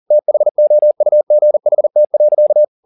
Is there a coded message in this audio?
Yes: Morse code from 0.1 to 2.6 s, heard as a keyed tone, about 40 wpm, 600 Hz. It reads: THOUGHT.